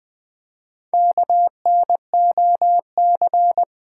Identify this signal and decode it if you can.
Morse code, heard as a keyed tone, about 20 wpm, 700 Hz. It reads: KNOC
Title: Build A Word - 1 Letter At A Time - 5-Letter Words - 20wpm